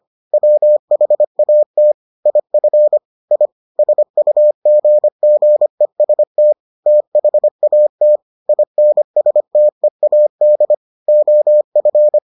Morse code, heard as a keyed tone, about 25 wpm, 600 Hz. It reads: WHAT IF I SUGGEST THAT INSTEAD OF